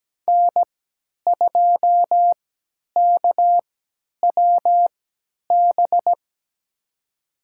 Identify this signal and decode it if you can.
Morse code, heard as a keyed tone, about 17 wpm, spaced as for 10 wpm, 700 Hz. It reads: N2KWB
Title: USA Call Signs - 5 Characters - 3x Character Spacing - 17wpm